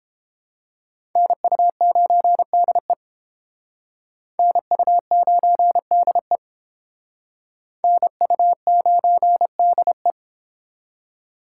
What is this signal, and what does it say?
Morse code, heard as a keyed tone, about 29 wpm, 700 Hz. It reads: NU9DE NU9DE NU9DE